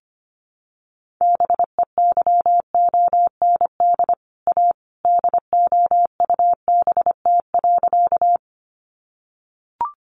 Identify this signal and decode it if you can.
Morse code, heard as a keyed tone, about 25 wpm, 700 Hz. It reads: BEYOND A DOUBT.
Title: Sentences from Top 1500 Words 25wpm